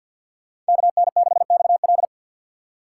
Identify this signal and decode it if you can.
Morse code, heard as a keyed tone, about 48 wpm, 700 Hz. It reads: KN6XL
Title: USA Call Signs - 5 Characters - 3x Character Spacing - 50wpm